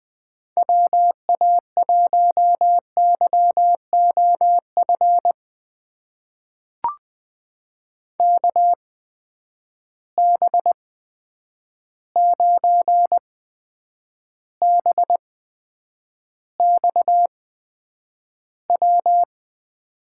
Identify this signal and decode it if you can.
Morse code, heard as a keyed tone, about 20 wpm, spaced as for 9 wpm, 700 Hz. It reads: WA1YOF KB9BXW
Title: USA Call Signs - 6 Characters - 8x Character Spacing - 20wpm